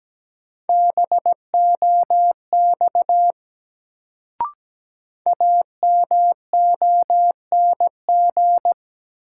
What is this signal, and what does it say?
Morse code, heard as a keyed tone, about 17 wpm, 700 Hz. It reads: BOX AMONG
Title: Sentences from Top 500 Words - New Word Drill - 17wpm